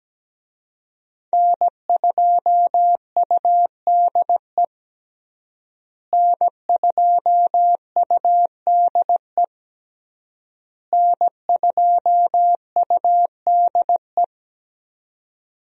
Morse code, heard as a keyed tone, about 17 wpm, 700 Hz. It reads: N2UDE N2UDE N2UDE
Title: USA Call Signs - 5 Characters - Repeated 3x - 17wpm